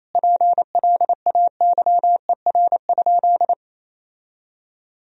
Morse code, heard as a keyed tone, about 28 wpm, 700 Hz. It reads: PLAYER?
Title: Sentences from Top 1000 Words - Review Part 4 - 28wpm